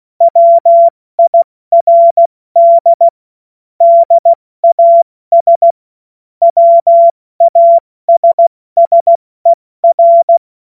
Morse code, heard as a keyed tone, about 16 wpm, 680 Hz. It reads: WIRD DAS WASSER
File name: Morsekurs / Witze / Witz 21. September 2021, 70 F80